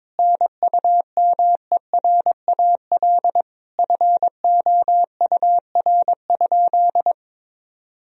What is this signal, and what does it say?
Morse code, heard as a keyed tone, about 22 wpm, 700 Hz. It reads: NUMERAL FOUR?
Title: Sentences from Top 1000 Words - Review Part 2 - 22wpm